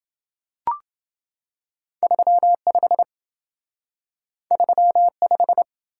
Morse code, heard as a keyed tone, about 28 wpm, 700 Hz. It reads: 35 35